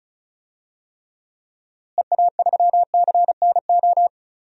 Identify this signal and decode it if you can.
Morse code, heard as a keyed tone, about 34 wpm, 700 Hz. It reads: EA3CNO
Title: DXCC-100 Call Signs - 6 Characters - Repeated 2x - 35wpm